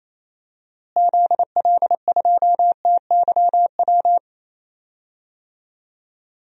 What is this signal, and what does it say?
Morse code, heard as a keyed tone, about 28 wpm, 700 Hz. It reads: ZL2TYW
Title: DXCC-100 Call Signs - 6 Characters - Repeated 2x - 28wpm